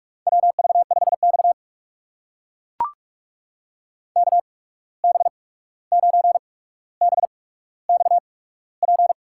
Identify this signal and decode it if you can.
Morse code, heard as a keyed tone, about 44 wpm, spaced as for 17 wpm, 700 Hz. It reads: WV5X KB9BXP
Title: USA Call Signs - 8x Character Spacing - 45wpm